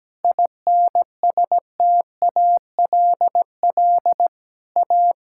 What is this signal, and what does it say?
Morse code, heard as a keyed tone, about 17 wpm, 700 Hz. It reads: INSTALL A
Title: Sentences from Top 1600 Words 17wpm